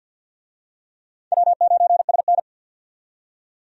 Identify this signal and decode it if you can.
Morse code, heard as a keyed tone, about 52 wpm, 700 Hz. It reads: W9SN
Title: USA Call Signs - 4 Characters - 50wpm